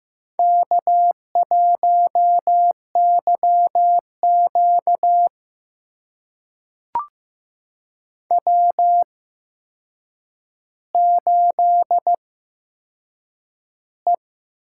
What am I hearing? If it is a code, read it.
Morse code, heard as a keyed tone, about 15 wpm, spaced as for 7 wpm, 700 Hz. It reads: K1YQ W8E
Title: USA Call Signs - 4 Characters - 8x Character Spacing - 15wpm